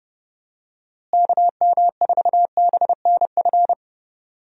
Morse code, heard as a keyed tone, about 30 wpm, 700 Hz. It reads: KM4BNF
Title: USA Call Signs - 6 Characters - Repeated 3x - 30wpm